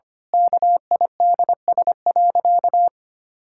Morse code, heard as a keyed tone, about 25 wpm, 700 Hz. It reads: KIDS.
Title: Pangram Challenge v2 - 25wpm